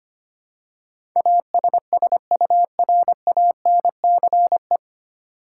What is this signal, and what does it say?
Morse code, heard as a keyed tone, about 25 wpm, 700 Hz. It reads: ASSURANCE
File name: Nine Letter Words - 25wpm